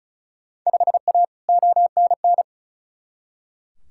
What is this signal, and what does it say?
Morse code, heard as a keyed tone, about 36 wpm, 700 Hz. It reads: 5A ONN
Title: Field Day Exchanges Encore - 35wpm